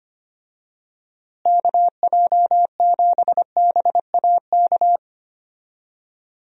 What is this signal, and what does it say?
Morse code, heard as a keyed tone, about 25 wpm, 700 Hz. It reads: KJ7BAK
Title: USA Call Signs - 6 Characters - Repeated 2x - 25wpm